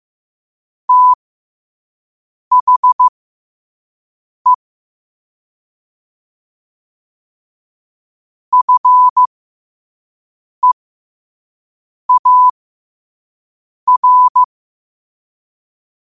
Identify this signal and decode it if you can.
Morse code, heard as a keyed tone, about 15 wpm, spaced as for 5 wpm, 1000 Hz. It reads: THE FEAR
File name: qotdfiles / 5wpm